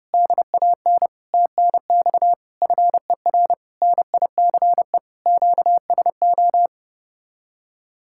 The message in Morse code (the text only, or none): DAN TNX FER NICE QSO